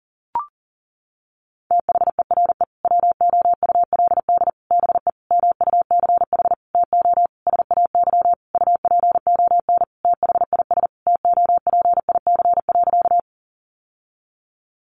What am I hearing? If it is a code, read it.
Morse code, heard as a keyed tone, about 37 wpm, 700 Hz. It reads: THERE WOULD BE MUCH TO SAY UPON THIS TOPIC.
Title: Sentences from Top 1200 Words 40wpm